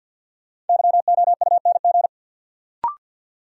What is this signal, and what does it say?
Morse code, heard as a keyed tone, about 47 wpm, 700 Hz. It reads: YOUNG
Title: Sentences from Top 400 Words - New Word Drill - 50wpm